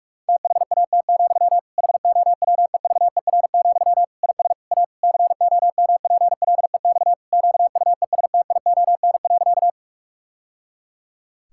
Morse code, heard as a keyed tone, about 41 wpm, 700 Hz. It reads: THAT, HOWEVER, IS A COMPLEX QUESTION.
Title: Sentences from Top 1000 Words - Review Part 6 - 45wpm